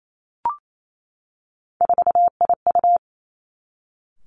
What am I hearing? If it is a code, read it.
Morse code, heard as a keyed tone, about 28 wpm, 700 Hz. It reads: 4IU
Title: Three Challenging Letter-Number Combinations 28wpm